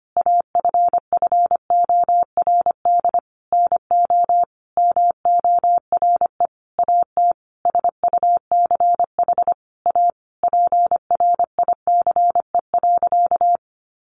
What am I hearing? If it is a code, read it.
Morse code, heard as a keyed tone, about 25 wpm, 700 Hz. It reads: AFFORD NO MORE AT SUCH A PRICE.
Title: Sentences from Top 1100 Words 25wpm